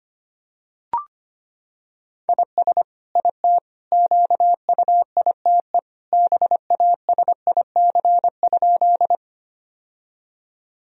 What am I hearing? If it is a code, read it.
Morse code, heard as a keyed tone, about 25 wpm, 700 Hz. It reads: IS IT QUITE BASIC?
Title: Sentences from Top 1000 Words - Review Part 3 - 25wpm